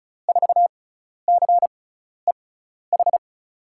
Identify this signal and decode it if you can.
Morse code, heard as a keyed tone, about 34 wpm, spaced as for 12 wpm, 700 Hz. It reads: 4CEH
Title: USA Call Signs - 6 Characters - 6x Character Spacing - 35wpm